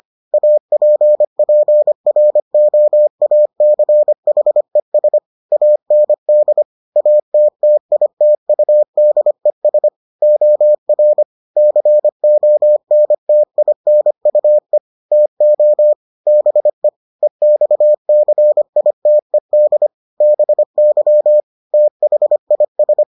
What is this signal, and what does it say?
Morse code, heard as a keyed tone, about 25 wpm, 600 Hz. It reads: APPROACHES AND ATTITUDES OR CONTINUE TO BE EXCITED BY THIS